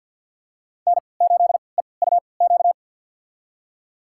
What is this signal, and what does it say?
Morse code, heard as a keyed tone, about 52 wpm, spaced as for 29 wpm, 700 Hz. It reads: N8EUQ
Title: USA Call Signs - 5 Characters - 3x Character Spacing - 50wpm